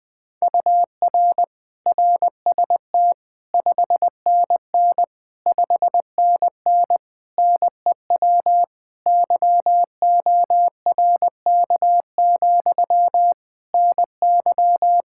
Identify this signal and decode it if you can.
Morse code, heard as a keyed tone, about 20 wpm, 700 Hz. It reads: UR RST 5NN 5NN NEW YORK, NY
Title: Practice QSO - Exchange 1 - Beginning - 20wpm